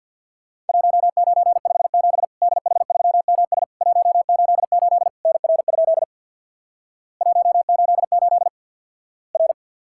Morse code, heard as a keyed tone, about 50 wpm, 700 Hz. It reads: 1957 DH3MS 188 NR? 188 R
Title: Worked All Europe DX Contest - QTC Exchange with Repeats - 50wpm